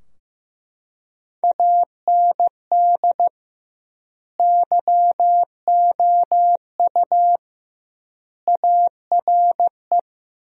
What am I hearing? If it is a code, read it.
Morse code, heard as a keyed tone, about 16 wpm, 700 Hz. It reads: AND YOU ARE
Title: 2x Word-Spacing - Sentences from Top 400 Words - 15wpm